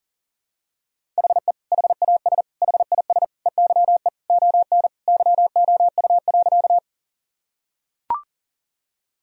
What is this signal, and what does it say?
Morse code, heard as a keyed tone, about 40 wpm, 700 Hz. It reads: HE HAS HIS EYE ON YOU.
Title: Sentences from Top 1000 Words - Review Part 1 - 40wpm